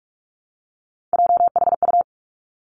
Morse code, heard as a keyed tone, about 46 wpm, 700 Hz. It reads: JHU
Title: Three Challenging Letter-Number Combinations 45wpm